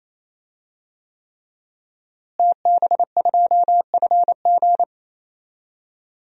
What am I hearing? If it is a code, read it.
Morse code, heard as a keyed tone, about 28 wpm, 700 Hz. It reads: TB2FG
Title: DXCC-100 Call Signs - 5 Characters - Repeated 3x - 28wpm